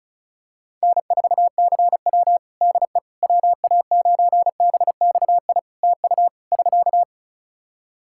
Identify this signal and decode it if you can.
Morse code, heard as a keyed tone, about 36 wpm, 700 Hz. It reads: N4CW DE WA9BXI TU <SK>
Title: Practice QSO - Exchange 4 - Ending - 35wpm